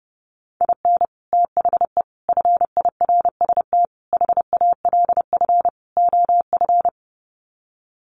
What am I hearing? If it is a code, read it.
Morse code, heard as a keyed tone, about 30 wpm, 700 Hz. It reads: IN THE FIRST HALF OF
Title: Sets of 5 Words 30wpm